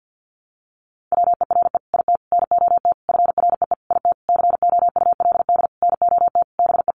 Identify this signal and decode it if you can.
Morse code, heard as a keyed tone, about 50 wpm, 700 Hz. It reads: WERE IT NOT FREE IT COULD NOT BE
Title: Sentences from Top 900 Words 50wpm